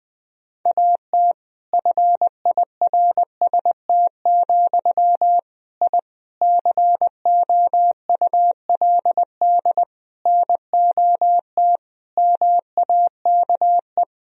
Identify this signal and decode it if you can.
Morse code, heard as a keyed tone, about 20 wpm, 700 Hz. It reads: AT FIRST, I COULD NOT MAKE
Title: Sentences from Top 1000 Words - Review Part 4 - 20wpm